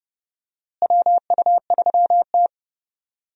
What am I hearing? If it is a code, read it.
Morse code, heard as a keyed tone, about 31 wpm, 700 Hz. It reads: WU3T
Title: USA Call Signs - 4 Characters - Repeated 2x - 30wpm